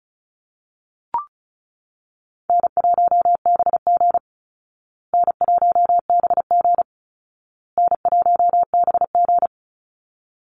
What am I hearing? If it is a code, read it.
Morse code, heard as a keyed tone, about 34 wpm, 700 Hz. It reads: N1BG N1BG N1BG
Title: Call Signs Repeated 2x - 35wpm